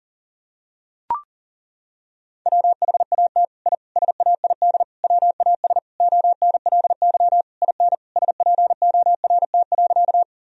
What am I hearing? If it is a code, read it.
Morse code, heard as a keyed tone, about 40 wpm, 700 Hz. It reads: WHAT I SAID WAS ONLY IN SPORT.
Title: Sentences from Top 1000 Words - Review Part 4 - 40wpm